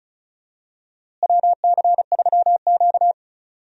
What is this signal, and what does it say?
Morse code, heard as a keyed tone, about 34 wpm, 700 Hz. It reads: WC3Q
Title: USA Call Signs - 4 Characters - Repeated 2x - 35wpm